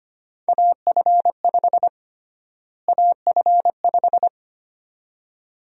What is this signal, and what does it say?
Morse code, heard as a keyed tone, about 25 wpm, 700 Hz. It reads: AF5 AF5
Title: USA Call Sign Prefixes - Repeated 2x - 25wpm